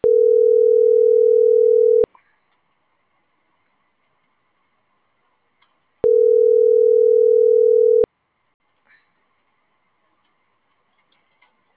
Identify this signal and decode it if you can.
Ringback tone (North America), 440 + 480 Hz.
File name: CENTRAL GENERAL / out yazna.cozumel